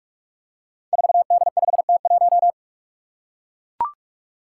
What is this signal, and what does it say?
Morse code, heard as a keyed tone, about 44 wpm, 700 Hz. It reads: 4D5T1
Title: Five Challenging Letter-Number Combinations 45wpm